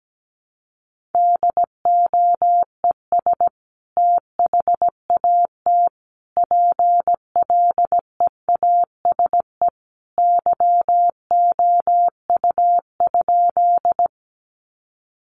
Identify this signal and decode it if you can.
Morse code, heard as a keyed tone, about 17 wpm, 700 Hz. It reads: DOES THAT PLEASE YOU?
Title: Sentences from Top 1500 Words 17wpm